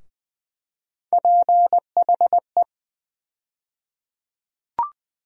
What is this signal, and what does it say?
Morse code, heard as a keyed tone, about 20 wpm, 700 Hz. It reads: PHE
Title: USA Call Sign Postfixes - 3 Letters 20wpm